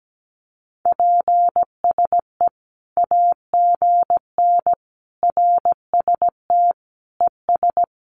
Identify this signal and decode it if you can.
Morse code, heard as a keyed tone, about 17 wpm, 700 Hz. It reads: PSE AGN RST ES
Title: Common QSO Phrases 17wpm